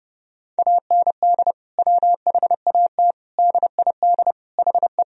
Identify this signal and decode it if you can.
Morse code, heard as a keyed tone, about 32 wpm, 700 Hz. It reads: AND WHAT DID HE
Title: Sentences from Top 1000 Words - Review Part 3 - 30wpm